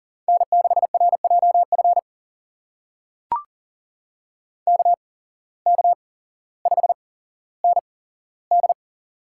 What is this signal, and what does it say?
Morse code, heard as a keyed tone, about 40 wpm, spaced as for 17 wpm, 700 Hz. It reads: N6RJF KK5ND
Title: USA Call Signs - 5 Characters - 8x Character Spacing - 40wpm